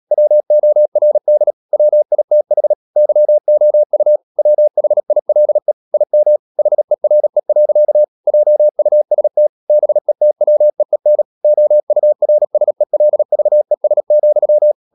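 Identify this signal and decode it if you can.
Morse code, heard as a keyed tone, about 37 wpm, 600 Hz. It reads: WORD WITH YOU WHILE IM HERE. JUST BETWEEN OURSELVES,
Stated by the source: Morse code practice at 35–39 wpm sending English text of Ken Follett's novel Never, from CW med Gnister.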